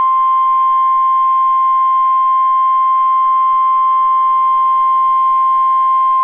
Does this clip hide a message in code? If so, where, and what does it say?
Yes: Emergency Alert System (SAME) tones (the 1050 Hz attention tone) from 0.6 to 5.9 s.